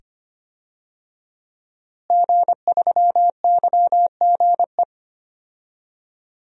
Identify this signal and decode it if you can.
Morse code, heard as a keyed tone, about 25 wpm, 700 Hz. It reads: G3YGE
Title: DXCC-100 Call Signs - 5 Characters - Repeated 2x - 25wpm